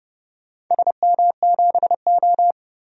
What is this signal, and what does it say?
Morse code, heard as a keyed tone, about 32 wpm, 700 Hz. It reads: SM7O
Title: DXCC-100 Call Signs - 4 Characters - 30wpm